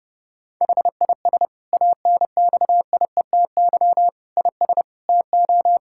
Morse code, heard as a keyed tone, about 30 wpm, 700 Hz. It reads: HIS ANXIETY IS TO
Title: Sentences from Top 1600 Words 30wpm